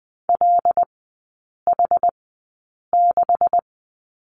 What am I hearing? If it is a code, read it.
Morse code, heard as a keyed tone, about 20 wpm, spaced as for 8 wpm, 700 Hz. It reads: LH6
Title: 159 Lesson 26e - Letter g - Three Characters 20wpm